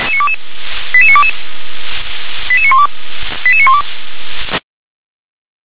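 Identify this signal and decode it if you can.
Five-tone selective call (ZVEI), heard as quick runs of five tones, tones of about 70 ms. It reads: 88922 88912 88912
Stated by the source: ZVEI five-tone selective call received by SDR.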